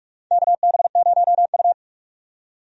Morse code, heard as a keyed tone, about 46 wpm, 700 Hz. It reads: KB0U
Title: USA Call Signs - 4 Characters - 45wpm